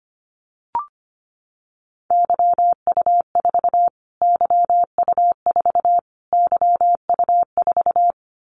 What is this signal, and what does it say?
Morse code, heard as a keyed tone, about 25 wpm, 700 Hz. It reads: YU4 YU4 YU4
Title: Callsign Prefixes for Top 25 DXCC - Repeated 3x - 25wpm